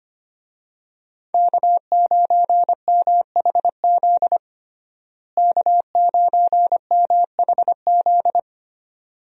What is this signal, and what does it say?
Morse code, heard as a keyed tone, about 25 wpm, 700 Hz. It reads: K9MHZ K9MHZ
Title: USA Call Signs - 5 Characters - Repeated 2x - 25wpm